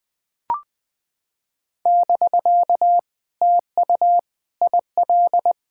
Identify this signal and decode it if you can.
Morse code, heard as a keyed tone, about 20 wpm, 700 Hz. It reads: <BK> TU IL
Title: POTA QSO Practice - 20wpm